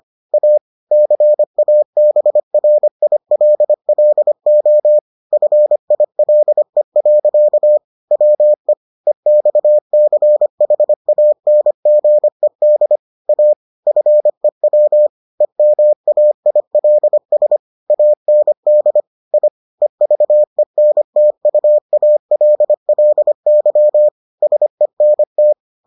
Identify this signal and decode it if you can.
Morse code, heard as a keyed tone, about 25 wpm, 600 Hz. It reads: A CABRILLO FILE. WE EXCHANGED A FEW EMAILS AND I EVENTUALLY SENT